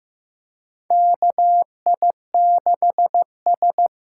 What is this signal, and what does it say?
Morse code, heard as a keyed tone, about 15 wpm, 700 Hz. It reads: KI6S
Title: USA Call Signs - 6 Characters - 8x Character Spacing - 15wpm